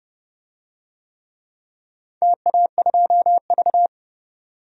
Morse code, heard as a keyed tone, about 31 wpm, 700 Hz. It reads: TA2V
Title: DXCC-100 Call Signs - 4 Characters - 30wpm